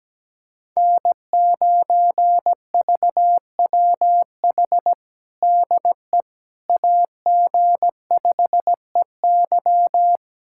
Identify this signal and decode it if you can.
Morse code, heard as a keyed tone, about 17 wpm, 700 Hz. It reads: N9VWH DE AG5EY
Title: USA Callsign DE Callsign - 5 Characters - 17wpm